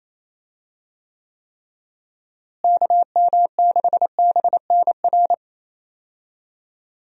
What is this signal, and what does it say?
Morse code, heard as a keyed tone, about 28 wpm, 700 Hz. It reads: KM6BNR
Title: USA Call Signs - 6 Characters - Repeated 3x - 28wpm